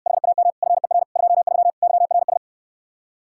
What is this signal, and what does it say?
Morse code, heard as a keyed tone, about 69 wpm, 700 Hz. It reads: HAM FER 23 YRS